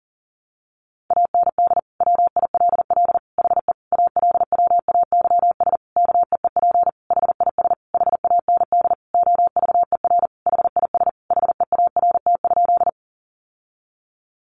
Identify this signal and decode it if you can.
Morse code, heard as a keyed tone, about 37 wpm, 700 Hz. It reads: AND WILL HE ALWAYS KEEP HIS HAND OVER HIS HEART?
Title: Sentences from Top 1200 Words 40wpm